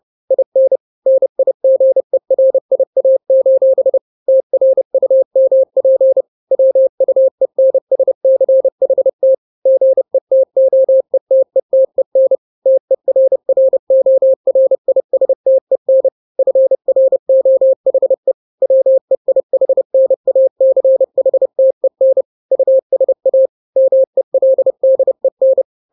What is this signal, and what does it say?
Morse code, heard as a keyed tone, about 30 wpm, 530 Hz. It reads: IN NIGERIA: TRUMP WUENSCHT GETOETETEN TERRORISTEN FROHE WEIHNACHTEN USA MELDEN
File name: 29wpm